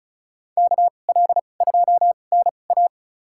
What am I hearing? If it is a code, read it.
Morse code, heard as a keyed tone, about 36 wpm, spaced as for 25 wpm, 700 Hz. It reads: KL2NA
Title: USA Call Signs - 2x Character Spacing - 35wpm